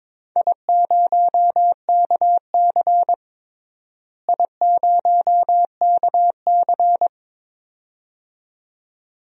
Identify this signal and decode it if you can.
Morse code, heard as a keyed tone, about 22 wpm, 700 Hz. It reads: I0KC I0KC
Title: DXCC-100 Call Signs - 4 Characters - Repeated 2x - 22wpm